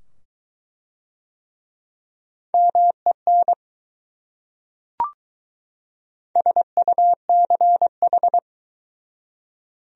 Morse code, heard as a keyed tone, about 22 wpm, 700 Hz. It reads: MEN SUCH